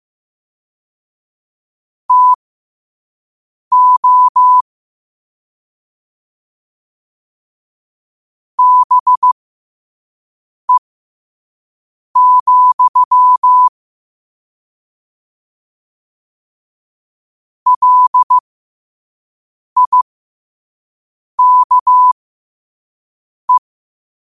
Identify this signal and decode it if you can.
Morse code, heard as a keyed tone, about 15 wpm, spaced as for 5 wpm, 1000 Hz. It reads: TO BE, LIKE
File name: 2026-02-15-5wpm